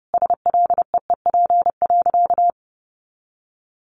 Morse code, heard as a keyed tone, about 30 wpm, 700 Hz. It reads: SLEEP.
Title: Sentences from Top 500 Words 30wpm